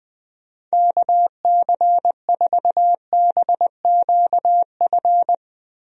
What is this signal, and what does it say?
Morse code, heard as a keyed tone, about 21 wpm, 700 Hz. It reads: KC4BQF
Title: USA Call Signs - 6 Characters - Repeated 3x - 20wpm